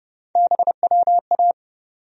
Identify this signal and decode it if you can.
Morse code, heard as a keyed tone, about 30 wpm, 700 Hz. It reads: BWA
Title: USA Call Sign Postfixes - 3 Letters 30wpm